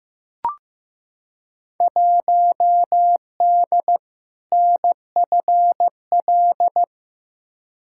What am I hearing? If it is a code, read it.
Morse code, heard as a keyed tone, about 15 wpm, 700 Hz. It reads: 1D NFL
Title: Field Day Exchanges - 15wpm